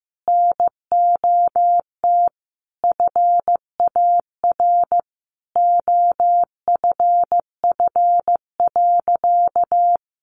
Morse code, heard as a keyed tone, about 15 wpm, 700 Hz. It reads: NOT FAR OFF.
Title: Sentences from Top 1400 Words 15wpm